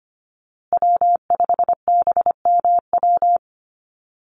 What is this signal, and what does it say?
Morse code, heard as a keyed tone, about 25 wpm, 700 Hz. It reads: W5BMW
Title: USA Call Signs Repeated 2x 25wpm